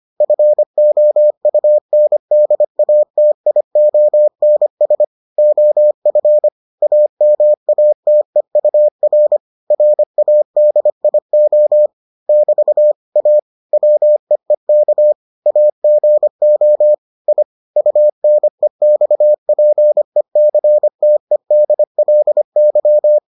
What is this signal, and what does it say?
Morse code, heard as a keyed tone, about 25 wpm, 600 Hz. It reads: FOUNDATIONS OF AMATEUR RADIO <BT> A WEEK AGO I UNEXPECTEDLY